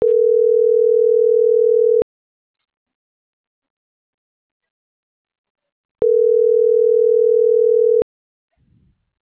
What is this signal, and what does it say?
Ringback tone (North America), 440 + 480 Hz.